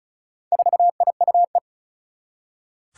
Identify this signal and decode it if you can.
Morse code, heard as a keyed tone, about 36 wpm, 700 Hz. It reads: 4IUE
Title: Four Challenging Letter-Number Combinations 35wpm